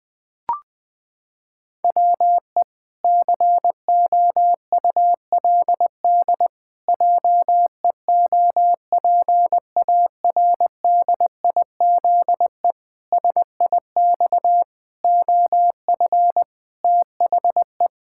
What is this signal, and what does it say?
Morse code, heard as a keyed tone, about 21 wpm, 700 Hz. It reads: WE COULD JEOPARDIZE SIX OF THE
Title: Pangram Challenge v2 - 20wpm